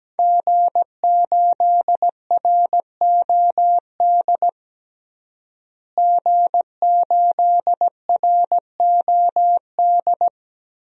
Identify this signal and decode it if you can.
Morse code, heard as a keyed tone, about 17 wpm, 700 Hz. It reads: G8ROD G8ROD
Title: DXCC-100 Call Signs - 5 Characters - Repeated 3x - 17wpm